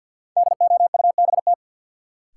Morse code, heard as a keyed tone, about 52 wpm, 700 Hz. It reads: DOUBT
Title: Sentences from Top 1500 Words - New Word Drill - 50wpm